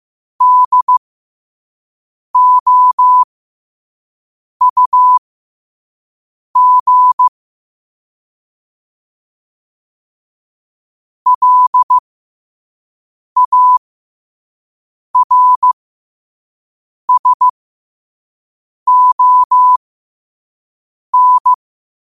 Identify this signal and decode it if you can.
Morse code, heard as a keyed tone, about 15 wpm, spaced as for 5 wpm, 1000 Hz. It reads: DOUG LARSON
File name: qotdfiles / 5wpm